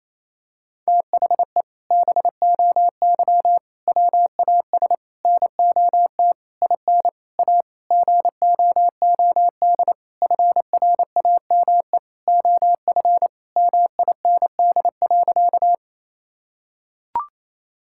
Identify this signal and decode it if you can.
Morse code, heard as a keyed tone, about 27 wpm, 700 Hz. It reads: THE BOY WAS NOT IN A GOOD FRAME OF MIND.
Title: Sentences from Top 1000 Words - Review Part 6 - 28wpm